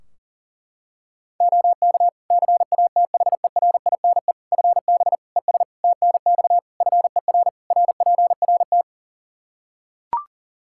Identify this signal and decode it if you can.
Morse code, heard as a keyed tone, about 40 wpm, 700 Hz. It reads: OK CATHERINE FB ES TNX FER RPRT
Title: Practice QSO - Exchange 2 - Info - 40wpm